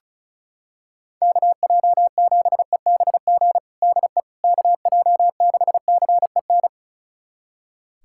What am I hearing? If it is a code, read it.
Morse code, heard as a keyed tone, about 34 wpm, 700 Hz. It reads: KJ7EBG DE KJ6CEN
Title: USA Callsign DE Callsign - 6 Characters - 35wpm